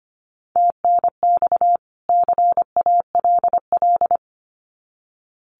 Morse code, heard as a keyed tone, about 25 wpm, 700 Hz. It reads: TNX CALL
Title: Common QSO Phrases 25wpm